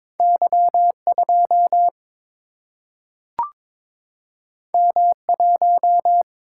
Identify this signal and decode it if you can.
Morse code, heard as a keyed tone, about 22 wpm, 700 Hz. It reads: Y2 M1
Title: Single-Letter Single-Number 22wpm